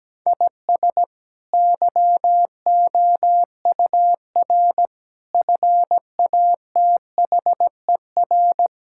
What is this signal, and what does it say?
Morse code, heard as a keyed tone, about 17 wpm, 700 Hz. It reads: IS YOUR FATHER
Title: Sentences from Top 1600 Words 17wpm